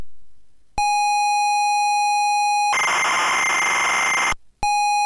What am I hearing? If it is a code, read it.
Computer data on cassette.